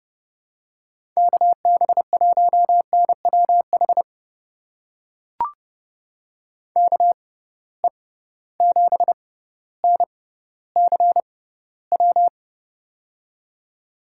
Morse code, heard as a keyed tone, about 30 wpm, spaced as for 15 wpm, 700 Hz. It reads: KB1NWH KE7NCW
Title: USA Call Signs - 6 Characters - 6x Character Spacing - 30wpm